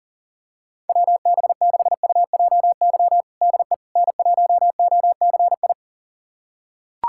Morse code, heard as a keyed tone, about 37 wpm, 700 Hz. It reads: WB6UJY DE N1OCI
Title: USA Callsign DE Callsign 40wpm